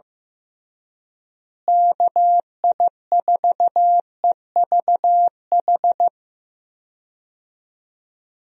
Morse code, heard as a keyed tone, about 15 wpm, 700 Hz. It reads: KI4EVH
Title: USA Call Signs - 6 Characters - Repeated 3x - 15wpm